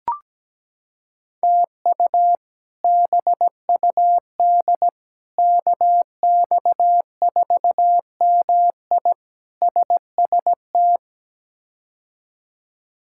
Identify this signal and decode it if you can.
Morse code, heard as a keyed tone, about 17 wpm, 700 Hz. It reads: TU BUD KX4MI SST
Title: K1USN SST QSO Practice - 17wpm